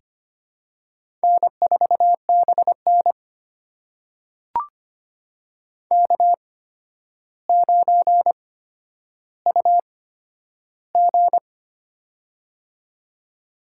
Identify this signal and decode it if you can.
Morse code, heard as a keyed tone, about 25 wpm, spaced as for 11 wpm, 700 Hz. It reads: N4BN K9UG
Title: USA Call Signs - 4 Characters - 8x Character Spacing - 25wpm